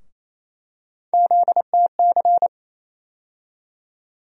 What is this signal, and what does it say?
Morse code, heard as a keyed tone, about 28 wpm, 700 Hz. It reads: ZTC
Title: USA Call Sign Postfixes - 3 Letters 28wpm